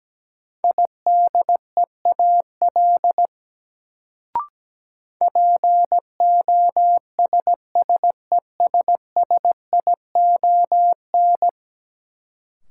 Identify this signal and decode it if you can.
Morse code, heard as a keyed tone, about 17 wpm, 700 Hz. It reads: IDEAL POSSESSION
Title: Sentences from Top 1400 Words - New Word Drill - 17wpm